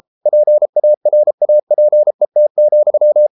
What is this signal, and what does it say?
Morse code, heard as a keyed tone, about 33 wpm, 600 Hz. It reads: PARAPET,